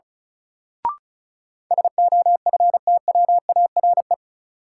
Morse code, heard as a keyed tone, about 36 wpm, 700 Hz. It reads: SOFTWARE
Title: Sentences from Top 1000 Words - Review Part 3 - New Word Drill - 35wpm